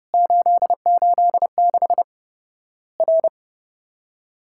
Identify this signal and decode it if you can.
Morse code, heard as a keyed tone, about 30 wpm, 700 Hz. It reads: 886 R
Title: Worked All Europe DX Contest - QTC Exchange - 30wpm